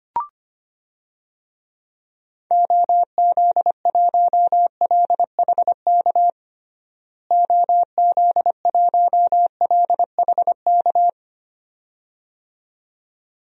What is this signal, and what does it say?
Morse code, heard as a keyed tone, about 25 wpm, 700 Hz. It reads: OZ1LHK OZ1LHK
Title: DXCC-100 Call Signs - 6 Characters - Repeated 2x - 25wpm